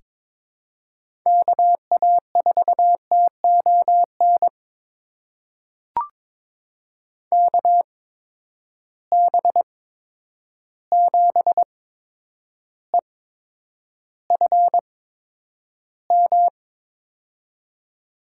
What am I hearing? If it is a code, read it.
Morse code, heard as a keyed tone, about 22 wpm, spaced as for 9 wpm, 700 Hz. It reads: KA4TON KB7EFM
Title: USA Call Signs - 6 Characters - 8x Character Spacing - 22wpm